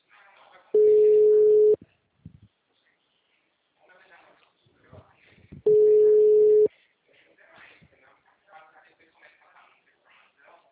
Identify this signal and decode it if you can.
Ringback tone (Europe, 425 Hz), with speech.
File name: out- -2005- - - .40452